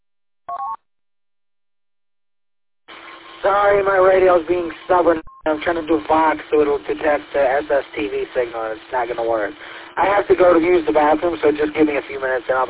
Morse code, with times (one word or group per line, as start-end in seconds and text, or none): none